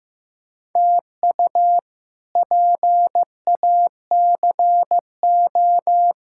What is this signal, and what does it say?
Morse code, heard as a keyed tone, about 15 wpm, 700 Hz. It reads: TU PACO
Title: K1USN SST QSO Practice - 15wpm